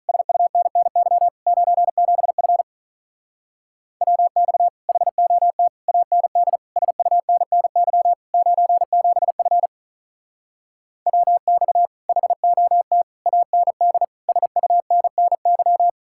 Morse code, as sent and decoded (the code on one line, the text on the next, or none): ... ..- -. -. -.-- / ----. --... ..-. / .-- -..- / .... --- - / .- -. -.. / ... ..- -. -. -.-- / ----. --... ..-. / .-- -..- / .... --- - / .- -. -.. / ... ..- -. -. -.--
SUNNY 97F WX HOT AND SUNNY 97F WX HOT AND SUNNY